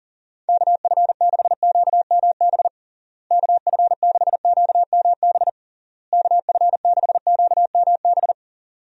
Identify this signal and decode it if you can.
Morse code, heard as a keyed tone, about 40 wpm, 700 Hz. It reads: KF6QMB KF6QMB KF6QMB
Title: USA Call Signs - 6 Characters - Repeated 3x - 40wpm